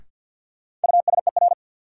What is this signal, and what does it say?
Morse code, heard as a keyed tone, about 52 wpm, 700 Hz. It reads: USER
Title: Sentences from Top 1000 Words - Review Part 5 - New Word Drill - 50wpm